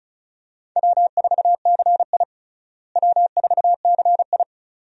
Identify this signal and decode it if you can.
Morse code, heard as a keyed tone, about 34 wpm, 700 Hz. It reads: W4CI W4CI
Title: USA Call Signs - 4 Characters - Repeated 2x - 35wpm